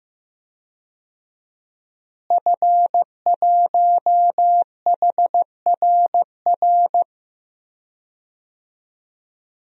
Morse code, heard as a keyed tone, about 15 wpm, 700 Hz. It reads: F1HRR
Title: DXCC-100 Call Signs - 5 Characters - Repeated 3x - 15wpm